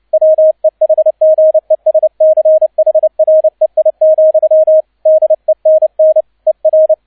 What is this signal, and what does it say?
Morse code, heard as a keyed tone, about 29 wpm, 630 Hz, with radio noise. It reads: WEHGESCHREI, DENN ER